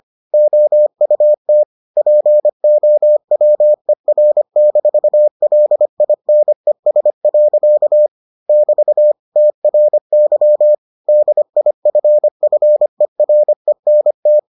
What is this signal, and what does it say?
Morse code, heard as a keyed tone, about 25 wpm, 600 Hz. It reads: OUT POWER-LINES. <BT> TRY DIFFERENT